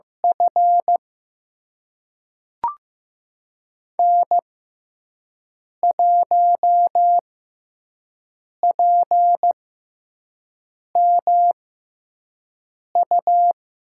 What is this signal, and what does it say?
Morse code, heard as a keyed tone, about 15 wpm, spaced as for 5 wpm, 700 Hz. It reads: F N1PMU
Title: USA Call Signs - 5 Characters - 6x Character Spacing - 15wpm